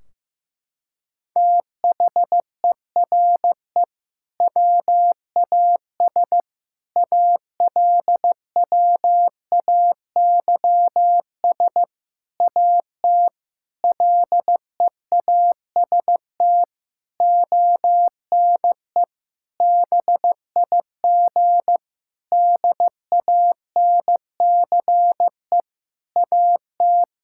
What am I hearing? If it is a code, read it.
Morse code, heard as a keyed tone, about 15 wpm, 700 Hz. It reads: THERE WAS ALWAYS AT LEAST ONE BIG DANCE AT
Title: Sentences from Top 1600 Words 15wpm